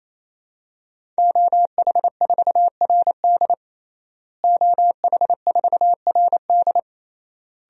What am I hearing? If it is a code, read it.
Morse code, heard as a keyed tone, about 28 wpm, 700 Hz. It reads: OH4RD OH4RD
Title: DXCC-100 Call Signs - 5 Characters - Repeated 3x - 28wpm